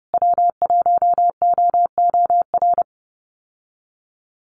Morse code, heard as a keyed tone, about 30 wpm, 700 Hz. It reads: W1OOR
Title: USA Call Signs Repeated 2x 30wpm